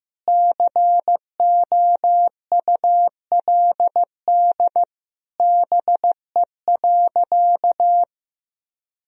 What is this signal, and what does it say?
Morse code, heard as a keyed tone, about 15 wpm, 700 Hz. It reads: COULD BE.
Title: Sentences from Top 1000 Words - Review Part 3 - 15wpm